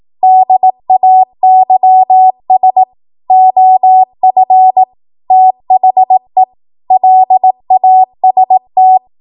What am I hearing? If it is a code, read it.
Morse code, heard as a keyed tone, about 18 wpm, 750 Hz. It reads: DAYS OF THE LAST